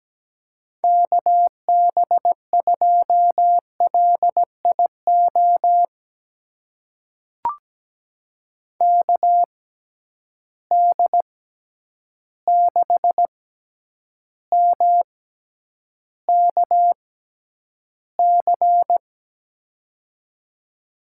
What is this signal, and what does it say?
Morse code, heard as a keyed tone, about 17 wpm, spaced as for 9 wpm, 700 Hz. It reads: KB2LIO KD6MKC
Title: USA Call Signs - 6 Characters - 6x Character Spacing - 17wpm